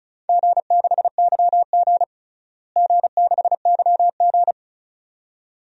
Morse code, heard as a keyed tone, about 36 wpm, 700 Hz. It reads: G6YG G6YG
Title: DXCC-100 Call Signs - 4 Characters - Repeated 2x - 35wpm